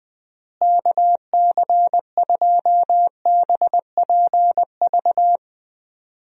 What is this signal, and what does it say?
Morse code, heard as a keyed tone, about 20 wpm, 700 Hz. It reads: KC2BPV
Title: USA Call Signs - 6 Characters - Repeated 2x - 20wpm